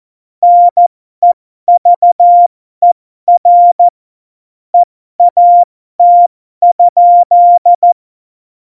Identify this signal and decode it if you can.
Morse code, heard as a keyed tone, about 14 wpm, spaced as for 12 wpm, 700 Hz. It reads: NEVER EAT?
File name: Morsekurs / Witze